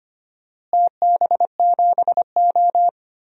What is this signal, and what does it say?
Morse code, heard as a keyed tone, about 25 wpm, 700 Hz. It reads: TB7O
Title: DXCC-100 Call Signs - 4 Characters - Repeated 2x - 25wpm